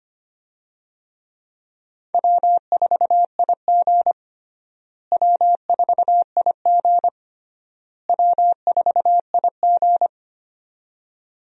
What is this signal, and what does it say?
Morse code, heard as a keyed tone, about 25 wpm, 700 Hz. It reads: W4IG W4IG W4IG
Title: USA Call Signs - 4 Characters - Repeated 3x - 25wpm